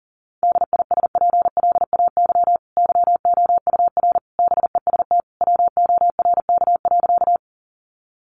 Morse code, heard as a keyed tone, about 40 wpm, 700 Hz. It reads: DISPLAY YOUR BEST WORK.
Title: Sentences from Top 1200 Words 40wpm